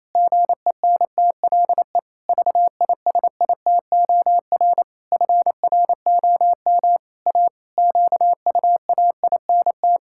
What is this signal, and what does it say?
Morse code, heard as a keyed tone, about 28 wpm, 700 Hz. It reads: GENTLE VISITOR FROM A QUAINT
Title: Pangram Challenge v2 - 28wpm